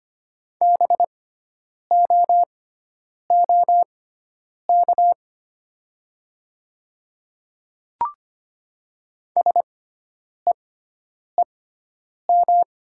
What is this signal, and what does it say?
Morse code, heard as a keyed tone, about 25 wpm, spaced as for 9 wpm, 700 Hz. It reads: BOOK SEEM
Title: 6x Character Spacing - Top 300 Words - 25wpm